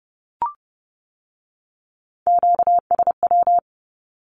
Morse code, heard as a keyed tone, about 30 wpm, 700 Hz. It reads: QSW
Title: Q-Codes with Spoken Meaning - 30wpm